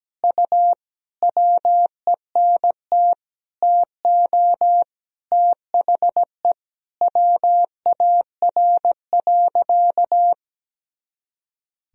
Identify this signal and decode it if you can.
Morse code, heard as a keyed tone, about 17 wpm, 700 Hz. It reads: U WENT TO THE WAR.
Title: Sentences from Top 400 Words and Abbreviations 17wpm